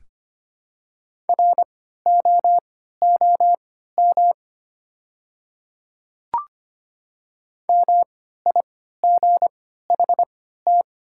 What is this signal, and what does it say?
Morse code, heard as a keyed tone, about 25 wpm, spaced as for 14 wpm, 700 Hz. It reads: ROOM MIGHT
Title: 3x Character Spacing - Top 300 Words - 25wpm